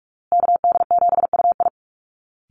Morse code, heard as a keyed tone, about 44 wpm, 700 Hz. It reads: KD7UI
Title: Call Signs Repeated 2x - 45wpm